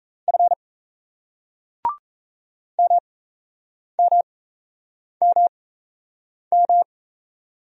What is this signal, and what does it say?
Morse code, heard as a keyed tone, about 38 wpm, spaced as for 8 wpm, 700 Hz. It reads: F MMMM